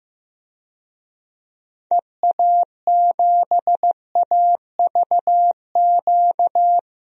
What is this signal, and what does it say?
Morse code, heard as a keyed tone, about 15 wpm, 700 Hz. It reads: EA7AVQ
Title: DXCC-100 Call Signs - 6 Characters - Repeated 3x - 15wpm